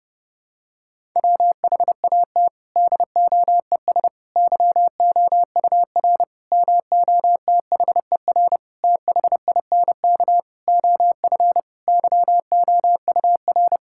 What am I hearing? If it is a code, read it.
Morse code, heard as a keyed tone, about 30 wpm, 700 Hz. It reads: WHAT DOES YOUR MOTHER THINK OF YOUR
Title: Sentences from Top 1000 Words - Review Part 5 - 30wpm